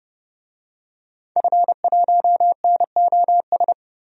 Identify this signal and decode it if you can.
Morse code, heard as a keyed tone, about 30 wpm, 700 Hz. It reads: F1NOS
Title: DXCC-100 Call Signs - 5 Characters - 30wpm